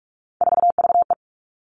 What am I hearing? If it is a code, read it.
Morse code, heard as a keyed tone, about 46 wpm, 700 Hz. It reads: 4VE
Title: Three Challenging Letter-Number Combinations 45wpm